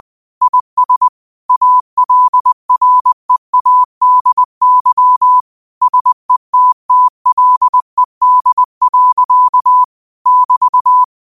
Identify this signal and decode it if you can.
Morse code, heard as a keyed tone, about 20 wpm, 1000 Hz. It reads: IS ALREADY SETTLED. <BT>
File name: qotdfiles / 20wpm